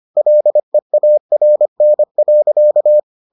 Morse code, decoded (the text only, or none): LEARN.